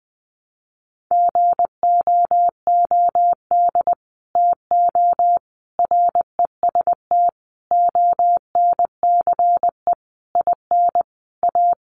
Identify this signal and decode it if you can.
Morse code, heard as a keyed tone, about 20 wpm, 700 Hz. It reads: GOOD TO REST ONCE IN A
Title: Sentences from Top 500 Words 20wpm